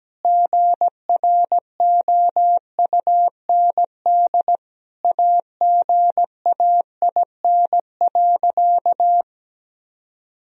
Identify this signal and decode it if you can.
Morse code, heard as a keyed tone, about 17 wpm, 700 Hz. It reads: GROUND AGAIN.
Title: Sentences from Top 1000 Words - Review Part 2 - 17wpm